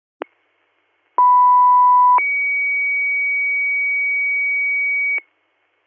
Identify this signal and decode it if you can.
Two-tone page, heard as a one-second tone, then a three-second tone, about 990 then 2250 Hz, with noise. It is heard from 1.2 to 5.2 s.